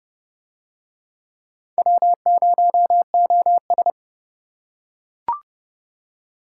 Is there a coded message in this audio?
Yes: Morse code from 1.8 to 3.9 s, heard as a keyed tone, about 26 wpm, 700 Hz. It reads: W0OS